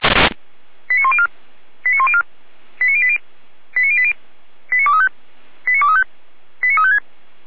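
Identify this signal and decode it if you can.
Five-tone selective call (ZVEI), heard as quick runs of five tones, tones of about 70 ms. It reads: 89104 89104 89080 89080 89235 89235 89356
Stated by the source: ZVEI five-tone selective call 89280 received with an SDR.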